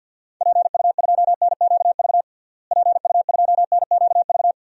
Morse code, heard as a keyed tone, about 46 wpm, 700 Hz. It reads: PU2NQV PU2NQV
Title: DXCC-100 Call Signs - 6 Characters - Repeated 3x - 50wpm